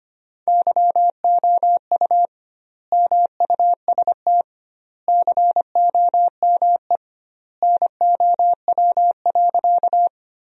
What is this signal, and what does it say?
Morse code, heard as a keyed tone, about 25 wpm, 700 Hz. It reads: YOU MUST COME NOW.
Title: 2x Word-Spacing - Sentences from Top 200 Words - 25wpm